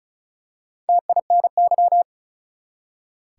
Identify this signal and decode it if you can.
Morse code, heard as a keyed tone, about 36 wpm, 700 Hz. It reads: TINY
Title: Four Letter Words - 35wpm